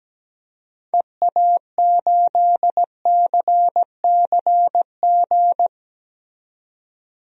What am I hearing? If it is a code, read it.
Morse code, heard as a keyed tone, about 17 wpm, 700 Hz. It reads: EA8CCG
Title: DXCC-100 Call Signs - 6 Characters - 17wpm